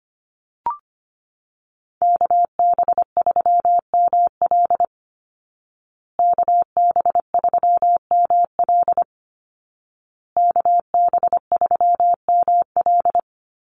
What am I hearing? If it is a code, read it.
Morse code, heard as a keyed tone, about 25 wpm, 700 Hz. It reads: KB3ML KB3ML KB3ML
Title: Call Signs Repeated 2x - 25wpm